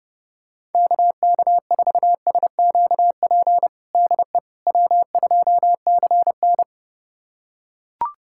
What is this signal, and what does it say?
Morse code, heard as a keyed tone, about 30 wpm, 700 Hz. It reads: KK4SQP DE W2CN
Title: USA Callsign DE Callsign 30wpm